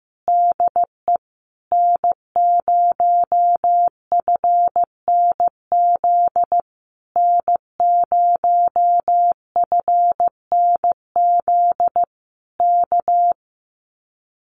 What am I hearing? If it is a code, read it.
Morse code, heard as a keyed tone, about 15 wpm, 700 Hz. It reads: DE N0FNZ N0FNZ K
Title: Common QSO Phrases Encore 15wpm